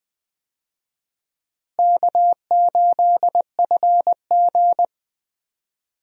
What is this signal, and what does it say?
Morse code, heard as a keyed tone, about 20 wpm, 700 Hz. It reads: K8FG
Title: USA Call Signs - 4 Characters - Repeated 3x - 20wpm